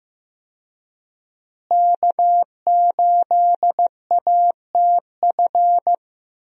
Morse code, heard as a keyed tone, about 15 wpm, 700 Hz. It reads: K8ATF
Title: USA Call Signs - 5 Characters - Repeated 2x - 15wpm